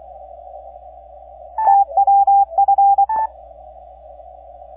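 Morse code, heard as a keyed tone, about 24 wpm, 800 Hz. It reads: TWF
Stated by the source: 3Y0K on Bouvet Island, DX on 10 meters CW, heard by N0UN in Denver during a weak opening that faded out.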